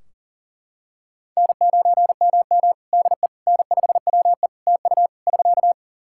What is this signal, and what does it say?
Morse code, heard as a keyed tone, about 40 wpm, 700 Hz. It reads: N9MM DE N5WE TU <SK>
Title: Practice QSO - Exchange 4 - Ending - 40wpm